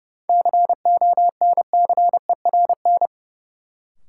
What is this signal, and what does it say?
Morse code, heard as a keyed tone, about 32 wpm, 700 Hz. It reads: CONCERN
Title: Sentences from Top 1000 Words - New Word Drill - 30wpm